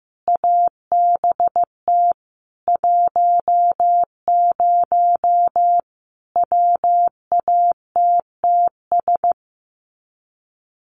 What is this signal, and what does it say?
Morse code, heard as a keyed tone, about 15 wpm, 700 Hz. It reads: ABT 10 WATTS
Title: Common QSO Phrases 15wpm